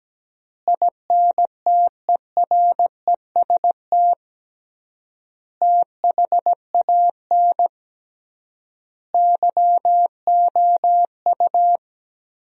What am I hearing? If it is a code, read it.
Morse code, heard as a keyed tone, about 17 wpm, 700 Hz. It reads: INTEREST THAN YOU
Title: 3x Word-Spacing - Sentences from Top 400 Words - 17wpm